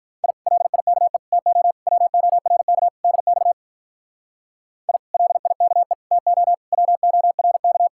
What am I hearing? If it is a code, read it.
Morse code, heard as a keyed tone, about 43 wpm, 700 Hz. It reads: I LIKE TO WORK DX I LIKE TO WORK
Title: Speed-Racing Common QSO Phrases - 35wpm